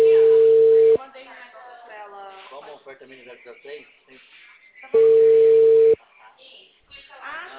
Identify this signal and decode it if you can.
Ringback tone (China, 450 Hz), with speech.
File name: out- -2015- - - .15222